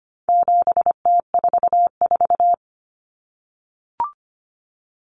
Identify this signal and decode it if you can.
Morse code, heard as a keyed tone, about 25 wpm, 700 Hz. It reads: 7T44
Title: Common Cut Numbers - 4-Digit 25wpm